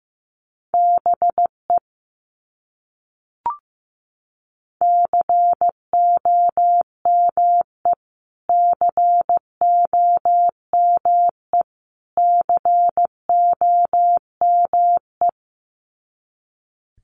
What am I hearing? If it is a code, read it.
Morse code, heard as a keyed tone, about 15 wpm, 700 Hz. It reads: BE COME COME COME
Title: Top 200 Words Repeated 15wpm